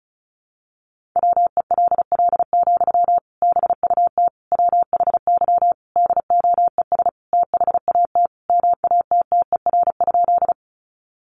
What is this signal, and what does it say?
Morse code, heard as a keyed tone, about 34 wpm, 700 Hz. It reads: WELL, BUT WHY DOES THAT MATTER?
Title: Sentences from Top 700 Words 35wpm